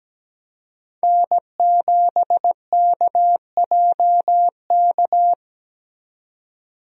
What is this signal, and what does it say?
Morse code, heard as a keyed tone, about 17 wpm, 700 Hz. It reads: N7KJK
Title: USA Call Signs - 5 Characters - 17wpm